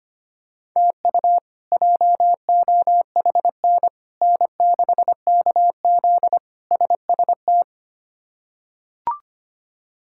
Morse code, heard as a keyed tone, about 25 wpm, 700 Hz. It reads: TU JOHN N6KZ SST
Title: K1USN SST QSO Practice - 25wpm